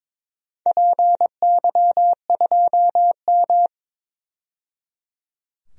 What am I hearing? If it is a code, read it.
Morse code, heard as a keyed tone, about 22 wpm, 700 Hz. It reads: PY2M
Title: DXCC-100 Call Signs - 4 Characters - 22wpm